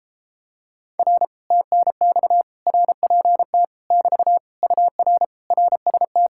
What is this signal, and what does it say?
Morse code, heard as a keyed tone, about 33 wpm, 700 Hz. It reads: R TNX RPT <BT> UR RST